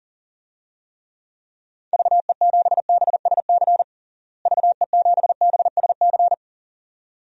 Morse code, heard as a keyed tone, about 40 wpm, 700 Hz. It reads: VE7BSC VE7BSC
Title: DXCC-100 Call Signs - 6 Characters - Repeated 2x - 40wpm